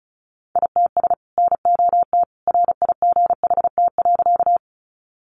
Morse code, heard as a keyed tone, about 34 wpm, 700 Hz. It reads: ITS NOT RIGHT.
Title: Sentences from Top 600 Words 35wpm